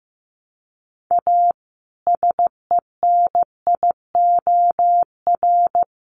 Morse code, heard as a keyed tone, about 15 wpm, 700 Hz. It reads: A SENIOR
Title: Sets of 5 Words Encore 15wpm